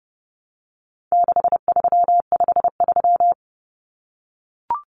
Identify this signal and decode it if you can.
Morse code, heard as a keyed tone, about 30 wpm, 700 Hz. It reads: 6353
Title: Numbers - 4-Digit 30wpm